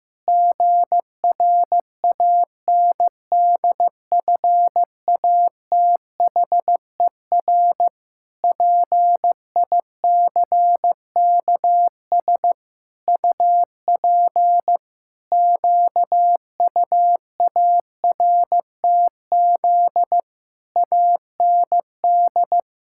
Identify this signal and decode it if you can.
Morse code, heard as a keyed tone, about 16 wpm, 700 Hz. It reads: GRANDFATHER PICKS UP QUARTZ AND
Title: Pangram Challenge v2 - 15wpm